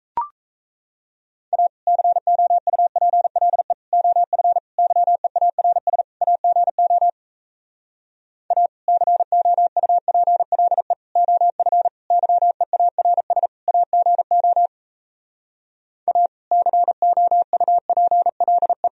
Morse code, heard as a keyed tone, about 38 wpm, 700 Hz. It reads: A COUPLE OF YEARS AGO A COUPLE OF YEARS AGO A COUPLE